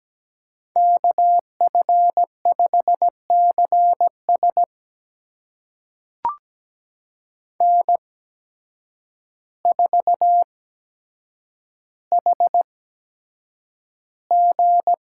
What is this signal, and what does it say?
Morse code, heard as a keyed tone, about 17 wpm, spaced as for 8 wpm, 700 Hz. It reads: KF5CS N4HG
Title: USA Call Signs - 8x Character Spacing - 17wpm